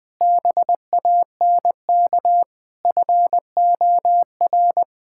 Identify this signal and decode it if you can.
Morse code, heard as a keyed tone, about 21 wpm, 700 Hz. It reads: BANK FOR
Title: Sentences from Top 1000 Words - Review Part 6 - 20wpm